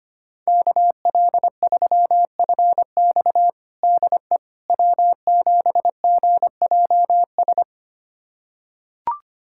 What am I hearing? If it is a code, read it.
Morse code, heard as a keyed tone, about 25 wpm, 700 Hz. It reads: KL3FX DE W7GJS
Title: USA Callsign DE Callsign - 5 Characters - 25wpm